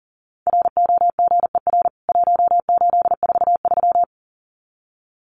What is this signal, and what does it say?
Morse code, heard as a keyed tone, about 40 wpm, 700 Hz. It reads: ROGER 1843
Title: CWT Exchange 40wpm